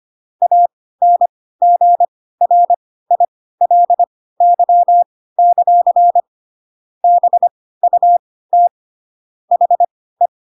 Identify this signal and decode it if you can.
Morse code, heard as a keyed tone, about 25 wpm, spaced as for 16 wpm, 700 Hz. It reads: ANGRILY; BUT HE
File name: Pinocchio / Pinocchio ch3 10 25 ik1hge